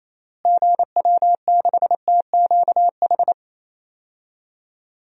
Morse code, heard as a keyed tone, about 27 wpm, 700 Hz. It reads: GW6TQH
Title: DXCC-100 Call Signs - 6 Characters - Repeated 2x - 28wpm